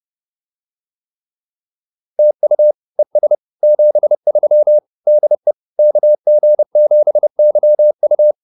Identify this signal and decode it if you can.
Morse code, heard as a keyed tone, about 30 wpm, 600 Hz. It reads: TU ES 73 DE KG7YU
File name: POTA108 30